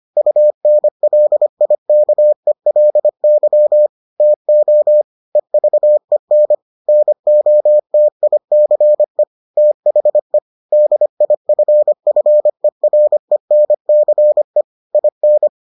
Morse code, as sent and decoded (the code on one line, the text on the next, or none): ..- -. .-.. .. -.- . .-.. -.-- / - --- / . ...- . -. / -. --- - .. -.-. / - .... / -.. .. ..-. ..-. .-. -. -.-. / .. -.
UNLIKELY TO EVEN NOTICE THE DIFFERENCE IN